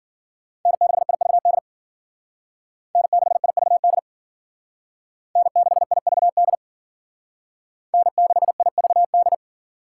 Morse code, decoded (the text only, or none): N6IVD N6IVD N6IVD N6IVD